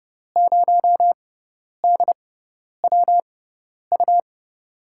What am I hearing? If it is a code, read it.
Morse code, heard as a keyed tone, about 30 wpm, spaced as for 10 wpm, 700 Hz. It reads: 0DWU
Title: USA Call Signs - 6 Characters - 6x Character Spacing - 30wpm